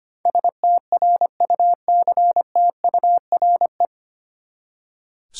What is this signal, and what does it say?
Morse code, heard as a keyed tone, about 25 wpm, 700 Hz. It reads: STRUCTURE
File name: Nine Letter Words - 25wpm